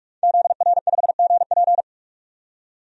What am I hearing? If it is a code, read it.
Morse code, heard as a keyed tone, about 44 wpm, 690 Hz. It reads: ZR5GP
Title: DXCC-100 Call Signs - 5 Characters - 45wpm